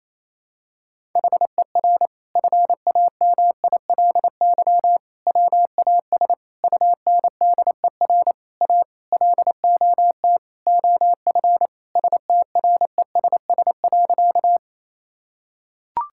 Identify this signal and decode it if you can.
Morse code, heard as a keyed tone, about 28 wpm, 700 Hz. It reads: HER FAMILY WAS UNDER A LOT OF STRESS.
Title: Sentences from Top 1000 Words - Review Part 6 - 28wpm